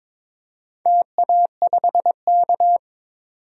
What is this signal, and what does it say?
Morse code, heard as a keyed tone, about 22 wpm, 700 Hz. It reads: TA5K
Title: DXCC-100 Call Signs - 4 Characters - 22wpm